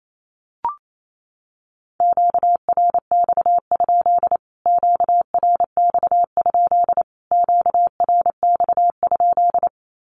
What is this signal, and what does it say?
Morse code, heard as a keyed tone, about 28 wpm, 700 Hz. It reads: QRX? QRX? QRX?
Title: Q-Codes Repeated 3x 28wpm